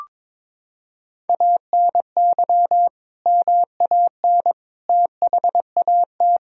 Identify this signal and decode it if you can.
Morse code, heard as a keyed tone, about 21 wpm, 700 Hz. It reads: ANY MAN THAT
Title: Sentences from Top 1000 Words - Review Part 1 - 22wpm